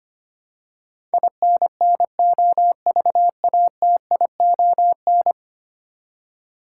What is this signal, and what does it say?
Morse code, heard as a keyed tone, about 25 wpm, 700 Hz. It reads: INNOVATION
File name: Ten Letter Words - 25wpm